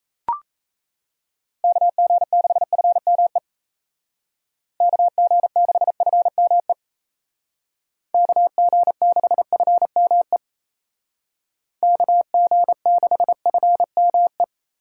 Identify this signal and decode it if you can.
Morse code, heard as a keyed tone, about 33 wpm, 700 Hz. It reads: KG6FME KG6FME KG6FME KG6FME
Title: Speed Racing - USA 2x3 Call Signs 28wpm